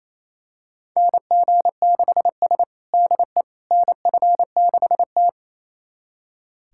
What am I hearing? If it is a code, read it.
Morse code, heard as a keyed tone, about 28 wpm, 700 Hz. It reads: NG6S DE NF6T
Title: USA Callsign DE Callsign - 4 Characters - 28wpm